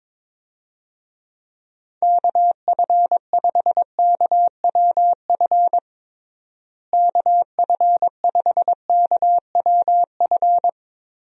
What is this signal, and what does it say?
Morse code, heard as a keyed tone, about 22 wpm, 700 Hz. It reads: KF5KWF KF5KWF
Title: USA Call Signs - 6 Characters - Repeated 2x - 22wpm